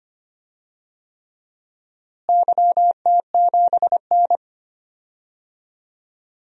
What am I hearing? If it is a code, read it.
Morse code, heard as a keyed tone, about 25 wpm, 700 Hz. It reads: YT7N
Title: DXCC-100 Call Signs - 4 Characters - Repeated 3x - 25wpm